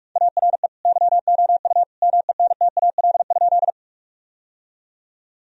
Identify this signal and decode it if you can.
Morse code, heard as a keyed tone, about 46 wpm, 700 Hz. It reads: ARE YOU MENTAL?
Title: Sentences from Top 1000 Words - Review Part 4 - 45wpm